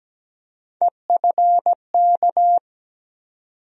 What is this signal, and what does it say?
Morse code, heard as a keyed tone, about 17 wpm, 700 Hz. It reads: EFK
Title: USA Call Sign Postfixes - 3 Letters 17wpm